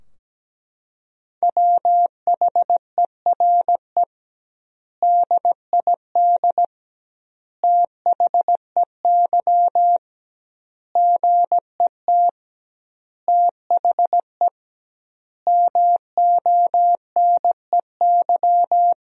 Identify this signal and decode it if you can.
Morse code, heard as a keyed tone, about 17 wpm, 700 Hz. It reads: WHERE DID THEY GET THE MONEY
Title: 2x Word-Spacing - Sentences from Top 400 Words - 17wpm